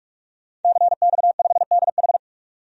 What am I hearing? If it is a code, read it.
Morse code, heard as a keyed tone, about 42 wpm, 700 Hz. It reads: CX5DH
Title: DXCC-100 Call Signs - 5 Characters - 45wpm